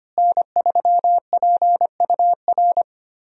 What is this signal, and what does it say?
Morse code, heard as a keyed tone, about 25 wpm, 700 Hz. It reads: N3PUR
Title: USA Call Signs - 5 Characters - Repeated 3x - 25wpm